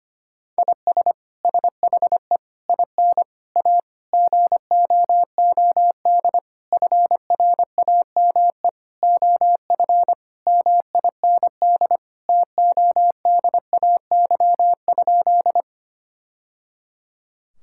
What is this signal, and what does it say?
Morse code, heard as a keyed tone, about 25 wpm, 700 Hz. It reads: IS SHE IN A GOOD FRAME OF MIND TODAY?
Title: Sentences from Top 1000 Words - Review Part 6 - 25wpm